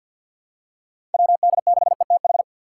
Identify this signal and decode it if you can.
Morse code, heard as a keyed tone, about 53 wpm, 700 Hz. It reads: WD6ETH
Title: USA Call Signs - 2x Character Spacing - 50wpm